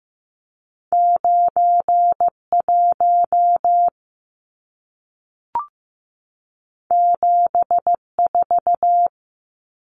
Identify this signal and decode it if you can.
Morse code, heard as a keyed tone, about 15 wpm, 700 Hz. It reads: 91 74
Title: Numbers - 2-Digit 15wpm